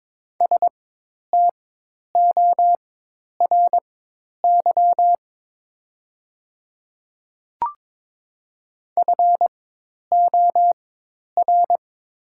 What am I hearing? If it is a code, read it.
Morse code, heard as a keyed tone, about 23 wpm, spaced as for 10 wpm, 700 Hz. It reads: STORY FOR
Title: 4x Character Spacing - Top 300 Words - 22wpm